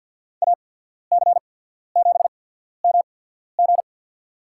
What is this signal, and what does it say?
Morse code, heard as a keyed tone, about 54 wpm, spaced as for 14 wpm, 700 Hz. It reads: AC7MG
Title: USA Call Signs - 5 Characters - 8x Character Spacing - 50wpm